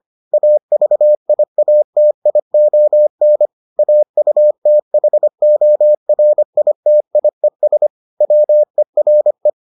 Morse code, heard as a keyed tone, about 25 wpm, 600 Hz. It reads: AVIATION AUTHORITIES WERE